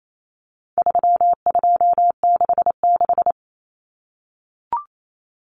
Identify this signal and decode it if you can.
Morse code, heard as a keyed tone, about 28 wpm, 700 Hz. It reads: 3266
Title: Numbers - 4-Digit 28wpm